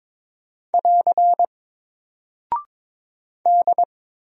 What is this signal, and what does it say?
Morse code, heard as a keyed tone, about 22 wpm, 700 Hz. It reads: <AR> D
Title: Single Character (Letter:Number:Punctuation:Prosign) Alt 22wpm